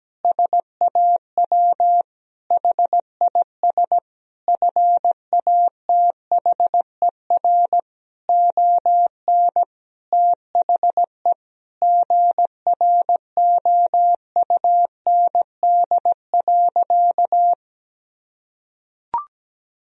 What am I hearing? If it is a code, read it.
Morse code, heard as a keyed tone, about 17 wpm, 700 Hz. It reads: SAW HIS FATHER ON THE GROUND.
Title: Sentences from Top 1000 Words - Review Part 3 - 17wpm